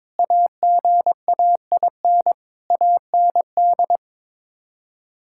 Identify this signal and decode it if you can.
Morse code, heard as a keyed tone, about 22 wpm, 700 Hz. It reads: AGAIN AND
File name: Sets of 9 Words - 22wpm